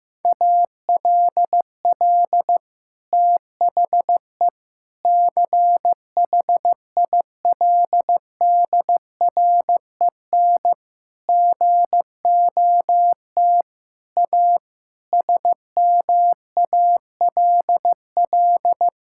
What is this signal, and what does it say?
Morse code, heard as a keyed tone, about 15 wpm, 700 Hz. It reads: ALL THE CHILDREN GOT A SMALL
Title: Sentences from Top 1000 Words - Review Part 6 - 15wpm